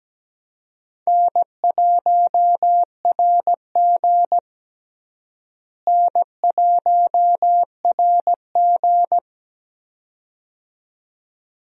Morse code, heard as a keyed tone, about 17 wpm, 700 Hz. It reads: N1RG N1RG
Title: USA Call Signs - 4 Characters - Repeated 2x - 17wpm